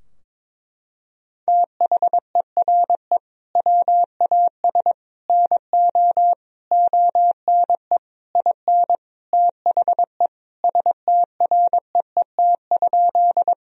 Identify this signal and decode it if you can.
Morse code, heard as a keyed tone, about 22 wpm, 700 Hz. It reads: THERE WAS NO ONE IN THE STREET?
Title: Sentences from Top 1000 Words - Review Part 3 - 22wpm